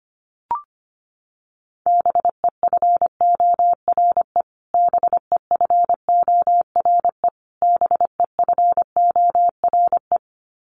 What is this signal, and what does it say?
Morse code, heard as a keyed tone, about 25 wpm, 700 Hz. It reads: BEFORE BEFORE BEFORE
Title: Top 200 Words Repeated 25wpm